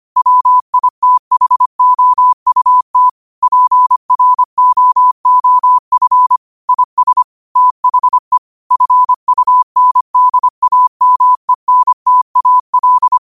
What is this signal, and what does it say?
Morse code, heard as a keyed tone, about 25 wpm, 1000 Hz. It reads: WITHOUT PROOF IS THE FUNDAMENTAL
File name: qotdfiles / 25wpm